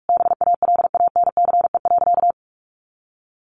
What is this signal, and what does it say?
Morse code, heard as a keyed tone, about 44 wpm, 700 Hz. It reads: BALANCE.
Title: Sentences from Top 900 Words 45wpm